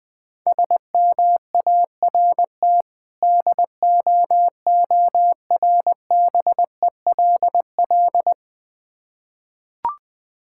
Morse code, heard as a keyed tone, about 20 wpm, 690 Hz. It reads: SMART DOORBELL
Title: Difficult Words with Hints - 20wpm